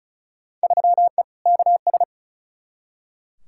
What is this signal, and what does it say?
Morse code, heard as a keyed tone, about 34 wpm, 700 Hz. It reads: 3E KS
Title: Field Day Exchanges Encore - 35wpm